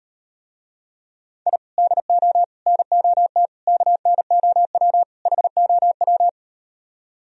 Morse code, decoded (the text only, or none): I DO NOT KNOW HOW